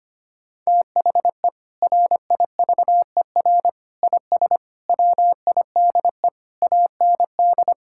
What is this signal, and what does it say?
Morse code, heard as a keyed tone, about 25 wpm, 700 Hz. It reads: THE RIVER IS WIDE AND
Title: Sentences from Top 1000 Words - Review Part 6 - 25wpm